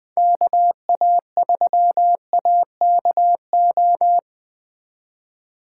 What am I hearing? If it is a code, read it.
Morse code, heard as a keyed tone, about 20 wpm, 700 Hz. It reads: KA3AKO
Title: Speed Racing - USA 2x3 Call Signs 17wpm